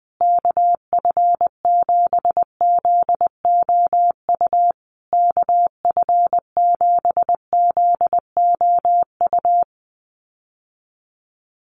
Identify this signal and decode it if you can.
Morse code, heard as a keyed tone, about 20 wpm, 700 Hz. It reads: KF7ZOU KF7ZOU
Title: USA Call Signs Repeated 3x 20wpm